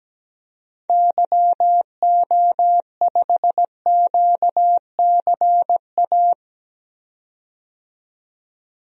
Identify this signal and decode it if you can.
Morse code, heard as a keyed tone, about 17 wpm, 700 Hz. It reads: YO5QCA
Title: DXCC-100 Call Signs - 6 Characters - 17wpm